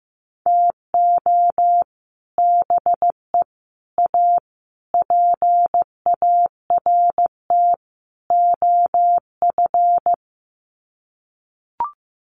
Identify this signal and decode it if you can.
Morse code, heard as a keyed tone, about 15 wpm, 700 Hz. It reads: TO BE A PART OF
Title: Sets of 5 Words 15wpm